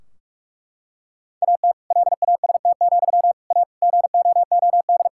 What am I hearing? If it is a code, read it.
Morse code, heard as a keyed tone, about 41 wpm, 700 Hz. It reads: AT LAST, A GOOD
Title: Sentences from Top 1600 Words 45wpm